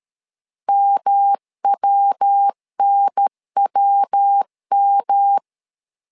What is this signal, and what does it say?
Morse code, heard as a keyed tone, about 13 wpm, 800 Hz. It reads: MWNWM
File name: LEZIONI / 3 MWRO / 3 MW 12wpm 1633